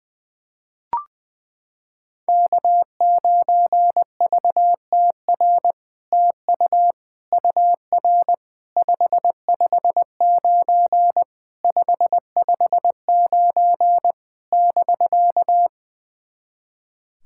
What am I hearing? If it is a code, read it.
Morse code, heard as a keyed tone, about 20 wpm, 700 Hz. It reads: K9VTR TU UR 559 559 <BK>
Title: SOTA QSO Practice 20wpm